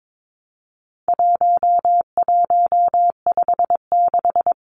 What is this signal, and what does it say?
Morse code, heard as a keyed tone, about 22 wpm, 700 Hz. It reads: 1156
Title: Numbers - 4-Digit 22wpm